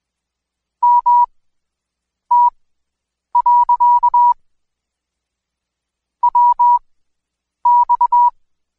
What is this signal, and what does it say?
Morse code, heard as a keyed tone, about 22 wpm, spaced as for 8 wpm, 970 Hz. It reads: MT. WX
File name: Code Quick / AudioDownld2 / Seg114-Day19-DayDrill 19